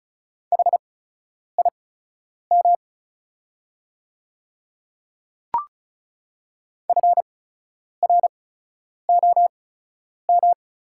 Morse code, heard as a keyed tone, about 36 wpm, spaced as for 10 wpm, 700 Hz. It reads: HIM FROM
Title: 8x Character Spacing - Top 100 Words - 35wpm